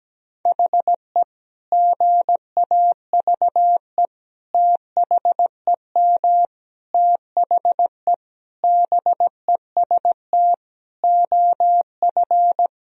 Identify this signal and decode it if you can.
Morse code, heard as a keyed tone, about 17 wpm, 700 Hz. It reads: HE GAVE THEM THE BEST OF
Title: Sentences from Top 1000 Words - Review Part 1 - 17wpm